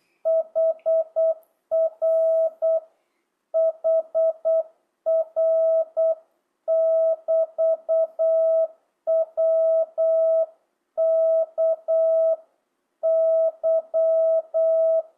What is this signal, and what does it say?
Morse code, heard as a keyed tone, about 8 wpm, 650 Hz. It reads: HR HR <BT> WK Y
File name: jh8cbh / 2025cw